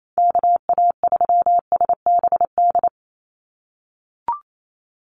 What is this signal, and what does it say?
Morse code, heard as a keyed tone, about 28 wpm, 700 Hz. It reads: KA3SBD
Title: USA Call Signs 28wpm